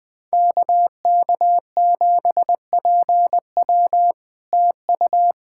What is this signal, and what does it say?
Morse code, heard as a keyed tone, about 20 wpm, 700 Hz. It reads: KK7PW TU
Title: SOTA QSO Practice 20wpm